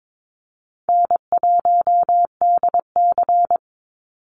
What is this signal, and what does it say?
Morse code, heard as a keyed tone, about 22 wpm, 700 Hz. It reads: N1DC
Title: Easy Call Signs - 22wpm